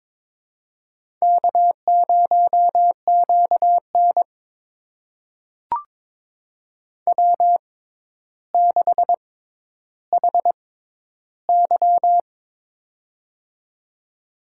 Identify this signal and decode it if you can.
Morse code, heard as a keyed tone, about 21 wpm, spaced as for 11 wpm, 700 Hz. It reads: K0QN W6HY
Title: USA Call Signs - 4 Characters - 6x Character Spacing - 22wpm